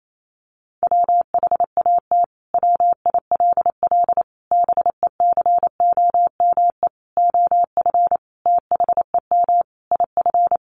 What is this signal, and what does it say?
Morse code, heard as a keyed tone, about 28 wpm, 700 Hz. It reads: WHAT WILL BECOME OF THEM IF
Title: Sentences from Top 1100 Words 28wpm